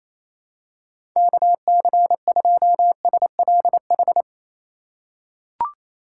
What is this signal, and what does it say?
Morse code, heard as a keyed tone, about 28 wpm, 700 Hz. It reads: KC2SLH
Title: USA Call Signs - 6 Characters - 3x Character Spacing - 28wpm